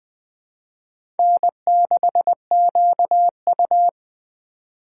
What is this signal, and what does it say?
Morse code, heard as a keyed tone, about 20 wpm, 700 Hz. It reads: N6QU
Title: USA Call Signs - 4 Characters - 3x Character Spacing - 20wpm